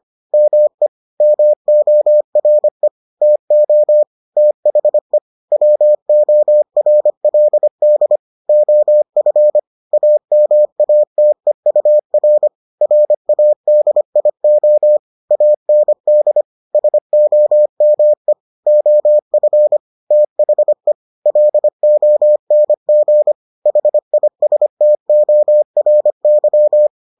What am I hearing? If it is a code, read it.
Morse code, heard as a keyed tone, about 25 wpm, 600 Hz. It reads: ME MORE TO THE WORLD OF AMATEUR RADIO AND SOME OF THE LONG HISTORY